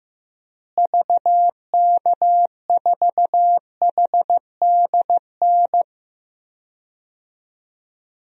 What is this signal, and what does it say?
Morse code, heard as a keyed tone, about 15 wpm, 700 Hz. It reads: VK4HDN
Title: DXCC-100 Call Signs - 6 Characters - Repeated 3x - 15wpm